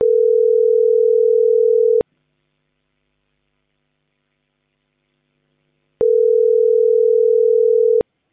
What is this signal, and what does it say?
Ringback tone (North America), 440 + 480 Hz.